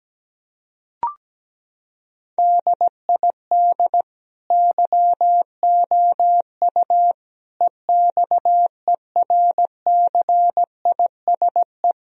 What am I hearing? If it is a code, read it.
Morse code, heard as a keyed tone, about 17 wpm, 700 Hz. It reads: DID YOU EXERCISE
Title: Sentences from Top 1000 Words - Review Part 4 - 17wpm